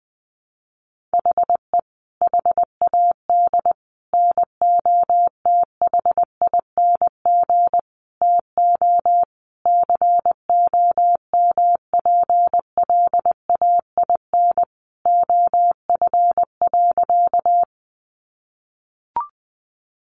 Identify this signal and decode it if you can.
Morse code, heard as a keyed tone, about 20 wpm, 700 Hz. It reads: HE HAD NOTHING TO COMPLAIN OF.
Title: Sentences from Top 1300 Words 20wpm